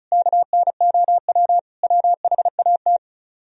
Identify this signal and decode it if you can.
Morse code, heard as a keyed tone, about 46 wpm, spaced as for 41 wpm, 700 Hz. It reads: KNOW WHAT
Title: Speed-Racing - Sets of 5 Words 35wpm